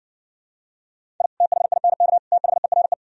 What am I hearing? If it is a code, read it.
Morse code, heard as a keyed tone, about 59 wpm, 700 Hz. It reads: I THINK THERE